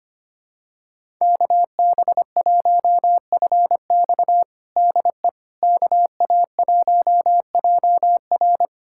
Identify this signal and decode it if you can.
Morse code, heard as a keyed tone, about 25 wpm, 700 Hz. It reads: KB1FX DE KA1JR
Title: USA Callsign DE Callsign - 5 Characters - 25wpm